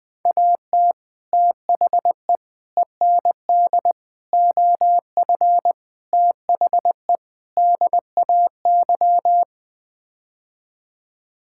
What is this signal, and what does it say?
Morse code, heard as a keyed tone, about 21 wpm, 700 Hz. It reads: AT THE END OF THE DAY
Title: Sets of 6 Words - 20wpm